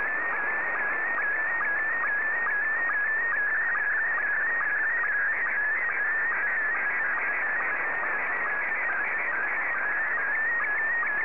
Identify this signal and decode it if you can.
SSTV (slow-scan television), Scottie 1.
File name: sstv / S1-VK2OY